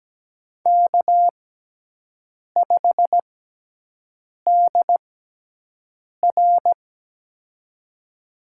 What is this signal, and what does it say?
Morse code, heard as a keyed tone, about 17 wpm, spaced as for 6 wpm, 700 Hz. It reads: K5DR
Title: USA Call Signs - 6x Character Spacing - 17wpm